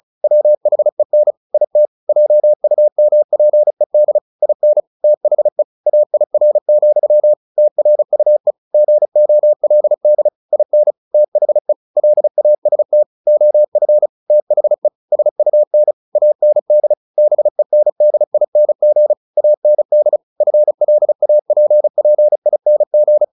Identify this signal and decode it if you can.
Morse code, heard as a keyed tone, about 34 wpm, 600 Hz. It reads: WHEN IT JUMPED IN THE AIR, TRUE GOLD IN THE LAST OF THE SUN AND BENDING AND FLAPPING
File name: the old man and the sea 2 35wpm